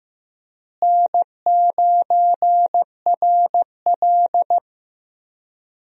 Morse code, heard as a keyed tone, about 15 wpm, 700 Hz. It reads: N9RL
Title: USA Call Signs - 4 Characters - Repeated 2x - 15wpm